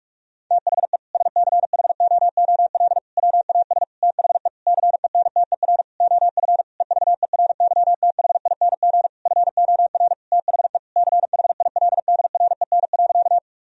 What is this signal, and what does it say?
Morse code, heard as a keyed tone, about 46 wpm, 700 Hz. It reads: THE SCHOOL WAS THE CENTER OF EVERYTHING FOR THE CHILDREN.
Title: Sentences from Top 1000 Words - Review Part 2 - 45wpm